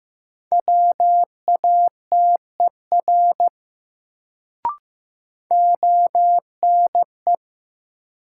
Morse code, heard as a keyed tone, about 15 wpm, 700 Hz. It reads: WATER ONE
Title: Sentences from Top 100 Words - New Word Drill - 15wpm